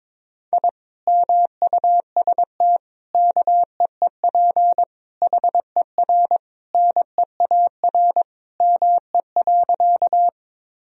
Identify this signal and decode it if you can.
Morse code, heard as a keyed tone, about 22 wpm, 700 Hz. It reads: I MUST KEEP HER NEAR ME.
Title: Sentences from Top 1000 Words - Review Part 1 - 22wpm